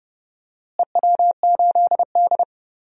Morse code, heard as a keyed tone, about 31 wpm, 700 Hz. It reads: EW8D
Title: DXCC-100 Call Signs - 4 Characters - Repeated 2x - 30wpm